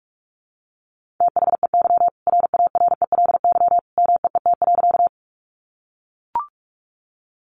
Morse code, heard as a keyed tone, about 44 wpm, 700 Hz. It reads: THEY RARELY MEET.
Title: Sentences from Top 1400 Words 45wpm